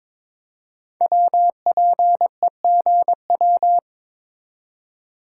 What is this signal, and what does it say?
Morse code, heard as a keyed tone, about 22 wpm, 690 Hz. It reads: WPEGW
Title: Code Groups - 5 Letter:Numbers - 22wpm